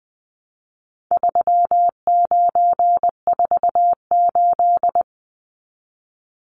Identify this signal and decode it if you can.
Morse code, heard as a keyed tone, about 20 wpm, 700 Hz. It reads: 3948
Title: Numbers - 4-Digit 20wpm